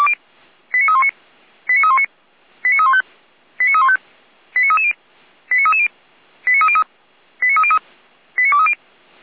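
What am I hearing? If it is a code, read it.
Five-tone selective call (ZVEI), heard as quick runs of five tones, tones of about 70 ms. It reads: 89319 89319 89315 89315 89330 89330 89303 89303 89230
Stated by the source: ZVEI five-tone selective call received on an SDR.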